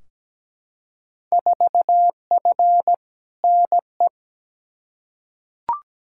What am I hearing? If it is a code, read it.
Morse code, heard as a keyed tone, about 17 wpm, 700 Hz. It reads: 4F NE
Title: Field Day Exchanges Encore - 17wpm